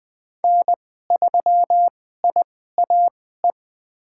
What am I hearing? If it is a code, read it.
Morse code, heard as a keyed tone, about 20 wpm, spaced as for 14 wpm, 700 Hz. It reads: N3IAE
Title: USA Call Signs - 5 Characters - 2x Character Spacing - 20wpm